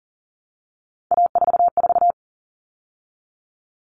Morse code, heard as a keyed tone, about 40 wpm, 700 Hz. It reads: A44
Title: All Cut Numbers - 3-Digit 40wpm